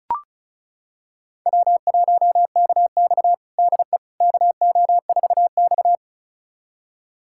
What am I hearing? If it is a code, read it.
Morse code, heard as a keyed tone, about 36 wpm, 700 Hz. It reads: W1KX DE KO4X
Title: USA Callsign DE Callsign - 4 Characters - 35wpm